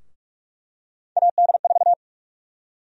Morse code, heard as a keyed tone, about 45 wpm, 700 Hz. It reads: AD4
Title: USA Call Sign Prefixes - Repeated 2x - 45wpm